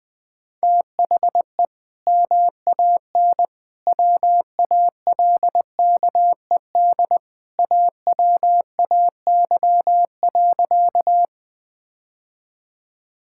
Morse code, heard as a keyed tone, about 20 wpm, 700 Hz. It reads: THE MAN WALKED AWAY.
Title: Sentences with 2-Word Phrases - 20wpm